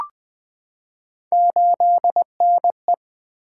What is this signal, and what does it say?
Morse code, heard as a keyed tone, about 20 wpm, 700 Hz. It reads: 8NE
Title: Three Letter-Number Combinations - 20wpm